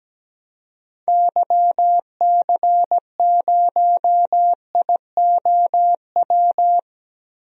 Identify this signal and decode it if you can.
Morse code, heard as a keyed tone, about 17 wpm, 700 Hz. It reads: YC0IOW
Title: DXCC-100 Call Signs - 6 Characters - Repeated 3x - 17wpm